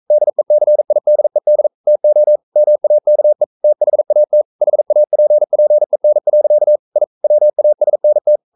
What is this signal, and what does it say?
Morse code, heard as a keyed tone, about 42 wpm, 600 Hz. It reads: DECIDED TO MAKE THAT HAPPEN. I WASNT
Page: Romanen Never af Ken Follet på engelsk.